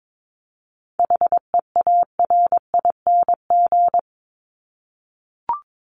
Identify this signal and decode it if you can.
Morse code, heard as a keyed tone, about 22 wpm, 700 Hz. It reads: HEARING
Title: ING Suffix - Top 100 Words with Base Words 22wpm